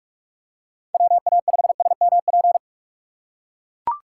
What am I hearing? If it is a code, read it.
Morse code, heard as a keyed tone, about 41 wpm, 700 Hz. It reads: WA5SMP
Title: USA Call Signs - 2x Character Spacing - 45wpm